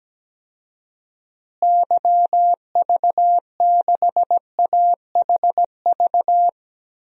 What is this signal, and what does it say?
Morse code, heard as a keyed tone, about 17 wpm, 700 Hz. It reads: YV6AHV
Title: DXCC-100 Call Signs - 6 Characters - Repeated 2x - 17wpm